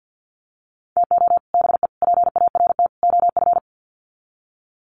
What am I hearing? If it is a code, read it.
Morse code, heard as a keyed tone, about 50 wpm, 700 Hz. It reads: TO BE PART OF
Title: Sets of 4 Words 50wpm